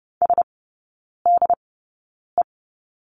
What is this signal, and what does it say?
Morse code, heard as a keyed tone, about 30 wpm, spaced as for 9 wpm, 700 Hz. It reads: SDE
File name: Lesson 17e - Letter u - Three Characters 30wpm